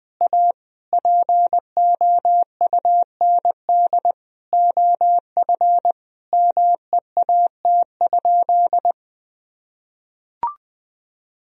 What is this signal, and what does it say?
Morse code, heard as a keyed tone, about 21 wpm, 700 Hz. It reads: A POUND OF MEAT?
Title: Sentences from Top 1000 Words - Review Part 3 - 20wpm